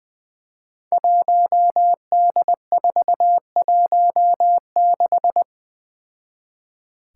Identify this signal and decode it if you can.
Morse code, heard as a keyed tone, about 20 wpm, 700 Hz. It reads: 1D416
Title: Five Challenging Letter-Number Combinations 20wpm